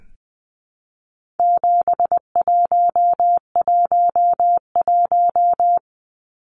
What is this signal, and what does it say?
Morse code, heard as a keyed tone, about 20 wpm, 700 Hz. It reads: 7111
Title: Numbers - 4-Digit 20wpm